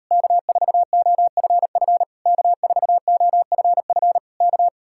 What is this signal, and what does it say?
Morse code, heard as a keyed tone, about 38 wpm, 700 Hz. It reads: K4OFF K4OFF K